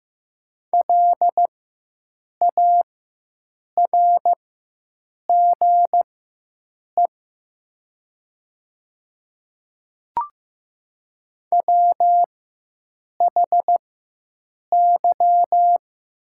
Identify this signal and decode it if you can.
Morse code, heard as a keyed tone, about 15 wpm, spaced as for 7 wpm, 700 Hz. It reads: LARGE WHY
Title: 4x Character Spacing - Top 300 Words - 15wpm